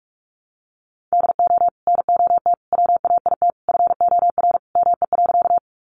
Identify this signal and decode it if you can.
Morse code, heard as a keyed tone, about 44 wpm, 700 Hz. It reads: DO NOT WAIT FOR ME.
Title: Sentences from Top 600 Words 45wpm